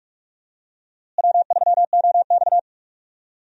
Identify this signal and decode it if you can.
Morse code, heard as a keyed tone, about 46 wpm, 700 Hz. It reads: W3OX
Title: USA Call Signs - 4 Characters - Repeated 2x - 45wpm